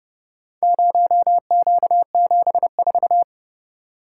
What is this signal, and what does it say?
Morse code, heard as a keyed tone, about 30 wpm, 700 Hz. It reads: 0Q74